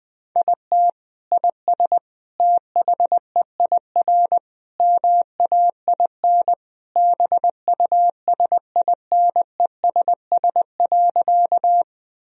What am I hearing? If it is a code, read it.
Morse code, heard as a keyed tone, about 20 wpm, 700 Hz. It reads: IT IS THEIR MAIN BUSINESS.
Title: Sentences from Top 1000 Words - Review Part 2 - 20wpm